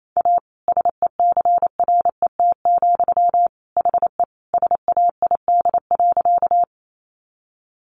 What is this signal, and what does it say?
Morse code, heard as a keyed tone, about 28 wpm, 700 Hz. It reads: A SECRET, HE SAID.
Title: Sentences from Top 1200 Words 28wpm